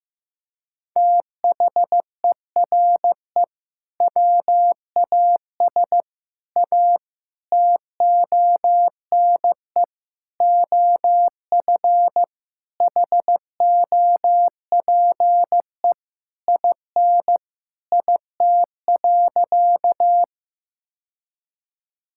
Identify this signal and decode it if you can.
Morse code, heard as a keyed tone, about 16 wpm, 700 Hz. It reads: THERE WAS A TONE OF HOPE IN IT.
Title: Sentences from Top 1600 Words 15wpm